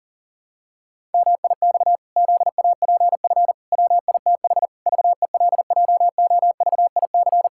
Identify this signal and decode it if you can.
Morse code, heard as a keyed tone, about 40 wpm, 700 Hz. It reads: MIX ZAPF WITH VELJOVIC
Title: Pangram Challenge v2 - 40wpm